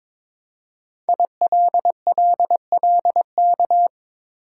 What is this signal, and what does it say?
Morse code, heard as a keyed tone, about 22 wpm, 700 Hz. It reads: ILLLK
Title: Code Groups - 5 Letters - 22wpm